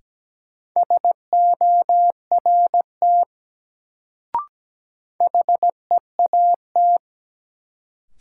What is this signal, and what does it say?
Morse code, heard as a keyed tone, about 17 wpm, 700 Hz. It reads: SORT HEAT
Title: Sentences from Top 1000 Words - Review Part 3 - New Word Drill - 17wpm